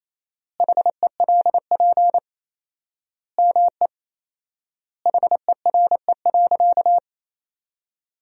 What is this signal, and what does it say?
Morse code, heard as a keyed tone, about 28 wpm, 700 Hz. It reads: HELP ME HERE.
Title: 4x Word-Spacing - Sentences from Top 200 Words - 28wpm